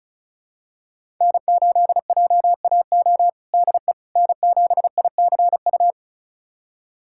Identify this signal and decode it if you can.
Morse code, heard as a keyed tone, about 36 wpm, 700 Hz. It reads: N8JAO DE N7ICU
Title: USA Callsign DE Callsign 35wpm